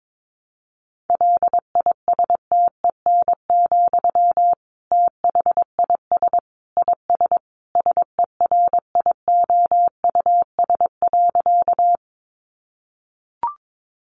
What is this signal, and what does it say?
Morse code, heard as a keyed tone, about 22 wpm, 700 Hz. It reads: LISTEN, THIS IS SERIOUS.
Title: Sentences from Top 1100 Words 22wpm